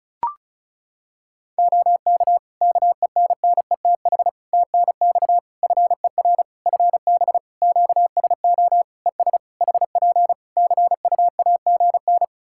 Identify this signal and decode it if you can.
Morse code, heard as a keyed tone, about 36 wpm, 700 Hz. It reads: OK KENNETH TNX FER FB QSO ES HP CUAGN
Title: Practice QSO - Exchange 4 - Ending - 35wpm